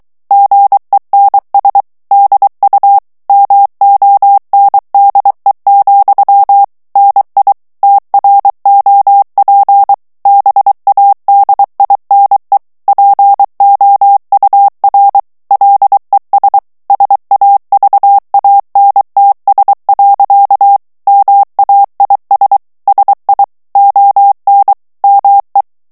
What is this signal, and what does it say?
Morse code, heard as a keyed tone, about 23 wpm, 800 Hz. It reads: GENS DU MONDE, NI TROP BADINE POUR LES SAVANTS. MAIS SI ON ME
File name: morse mp3 / FRANCAIS TEXTE CLAIR 2 VITESSE 22 MOTS MINUTES